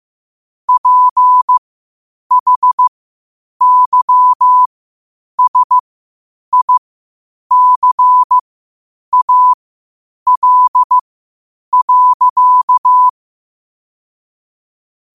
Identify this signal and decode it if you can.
Morse code, heard as a keyed tone, about 15 wpm, spaced as for 8 wpm, 1000 Hz. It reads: PHYSICAL.
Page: Quotes for Thu, 14 Aug 2025 in Morse Code at 8 words per minute.